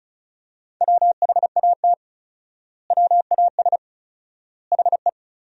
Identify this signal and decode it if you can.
Morse code, heard as a keyed tone, about 34 wpm, 700 Hz. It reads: WHAT WAS HE
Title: 4x Word-Spacing - Sentences from Top 100 Words - 35wpm